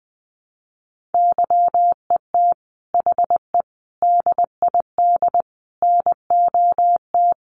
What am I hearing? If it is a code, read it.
Morse code, heard as a keyed tone, about 20 wpm, 700 Hz. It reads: YET HE DID NOT
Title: Sentences from Top 800 Words 20wpm